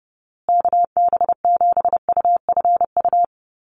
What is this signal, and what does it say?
Morse code, heard as a keyed tone, about 30 wpm, 700 Hz. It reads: KB7UFU
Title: USA Call Signs Repeated 3x 30wpm